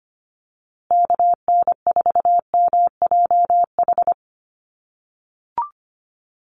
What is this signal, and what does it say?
Morse code, heard as a keyed tone, about 25 wpm, 700 Hz. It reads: KN4MJH
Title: USA Call Signs Repeated 2x 25wpm